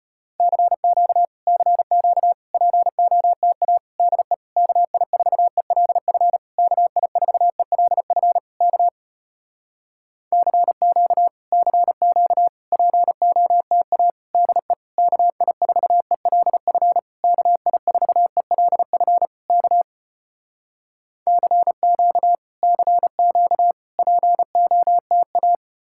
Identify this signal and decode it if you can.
Morse code, heard as a keyed tone, about 34 wpm, 700 Hz. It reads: CQ CQ POTA DE KI4ELF KI4ELF K CQ CQ POTA DE KI4ELF KI4ELF K CQ CQ POTA